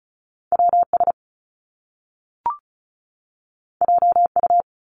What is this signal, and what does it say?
Morse code, heard as a keyed tone, about 36 wpm, 700 Hz. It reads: WS JU
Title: Top Two Letter Combinations in Words 35wpm